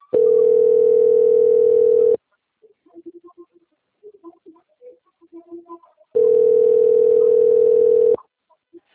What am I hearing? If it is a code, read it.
Ringback tone (North America), 440 + 480 Hz.